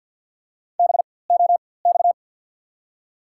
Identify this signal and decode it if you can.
Morse code, heard as a keyed tone, about 53 wpm, spaced as for 24 wpm, 700 Hz. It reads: BOX
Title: 4x Character Spacing - Top 500 Words - 50wpm